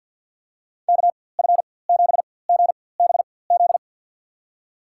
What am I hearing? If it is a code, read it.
Morse code, heard as a keyed tone, about 47 wpm, spaced as for 23 wpm, 700 Hz. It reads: KF7GBZ
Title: USA Call Signs - 6 Characters - 4x Character Spacing - 50wpm